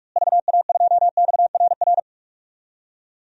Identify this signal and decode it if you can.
Morse code, heard as a keyed tone, about 46 wpm, 700 Hz. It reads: VA2XRR
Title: DXCC-100 Call Signs - 6 Characters - Repeated 3x - 45wpm